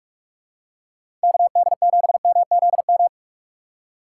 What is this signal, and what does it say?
Morse code, heard as a keyed tone, about 46 wpm, 700 Hz. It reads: KD7MZM
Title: USA Call Signs - 3x Character Spacing - 45wpm